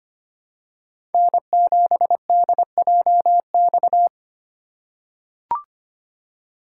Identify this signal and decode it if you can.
Morse code, heard as a keyed tone, about 25 wpm, 700 Hz. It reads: N7DJX
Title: USA Call Signs - 4x Character Spacing - 25wpm